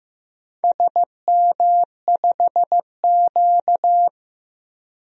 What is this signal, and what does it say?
Morse code, heard as a keyed tone, about 15 wpm, 700 Hz. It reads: SM5Q
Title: DXCC-100 Call Signs - 4 Characters - 15wpm